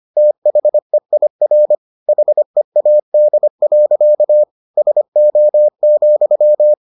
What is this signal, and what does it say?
Morse code, heard as a keyed tone, about 25 wpm, 600 Hz. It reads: THEIR HEAD. SO,